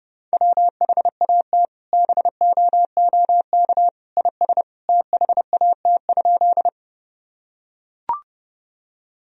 Morse code, heard as a keyed tone, about 30 wpm, 700 Hz. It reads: WHAT BOOK IS THAT?
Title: Sentences from Top 1000 Words - Review Part 1 - 30wpm